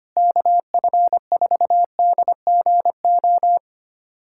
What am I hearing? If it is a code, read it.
Morse code, heard as a keyed tone, about 25 wpm, 700 Hz. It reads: KF4DGO
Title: USA Callsign DE Callsign - 6 Characters - 25wpm